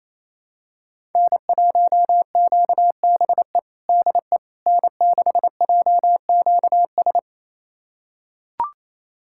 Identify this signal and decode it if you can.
Morse code, heard as a keyed tone, about 27 wpm, 700 Hz. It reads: N1QBE DE N6JQS
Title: USA Callsign DE Callsign - 5 Characters - 28wpm